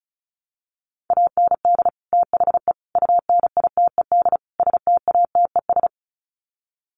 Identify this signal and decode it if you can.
Morse code, heard as a keyed tone, about 34 wpm, 700 Hz. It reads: AND THE UNITED STATES
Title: Sets of 4 Words 35wpm